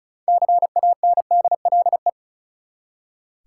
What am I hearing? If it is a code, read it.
Morse code, heard as a keyed tone, about 36 wpm, 700 Hz. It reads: CANDLE
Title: Six Letter Words - 35wpm